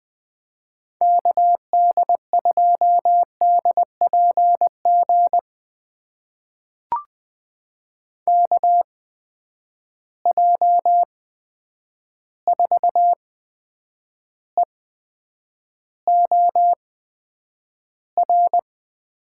Morse code, heard as a keyed tone, about 20 wpm, spaced as for 9 wpm, 700 Hz. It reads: KD2DPG KJ4EOR
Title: USA Call Signs - 6 Characters - 8x Character Spacing - 20wpm